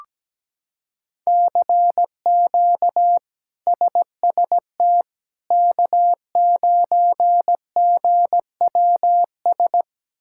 Morse code, heard as a keyed tone, about 17 wpm, 700 Hz. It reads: CQ SST K9GWS
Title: K1USN SST QSO Practice - 17wpm